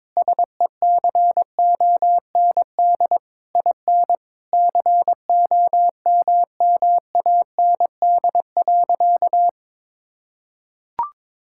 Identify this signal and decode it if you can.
Morse code, heard as a keyed tone, about 21 wpm, 700 Hz. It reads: SECOND IN COMMAND.
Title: Sentences from Top 1600 Words 22wpm